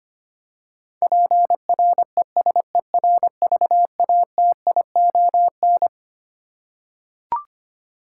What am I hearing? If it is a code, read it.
Morse code, heard as a keyed tone, about 25 wpm, 700 Hz. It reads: PRESERVATION
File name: Twelve Letter Words - 25wpm